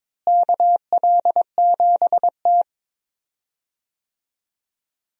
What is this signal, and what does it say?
Morse code, heard as a keyed tone, about 22 wpm, 700 Hz. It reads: KL7T
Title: DXCC-100 Call Signs - 4 Characters - Repeated 2x - 22wpm